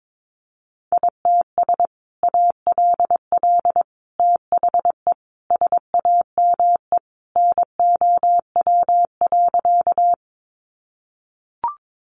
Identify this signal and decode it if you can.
Morse code, heard as a keyed tone, about 22 wpm, 700 Hz. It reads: ITS ALL THE SAME NOW.
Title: Sentences from Top 600 Words 22wpm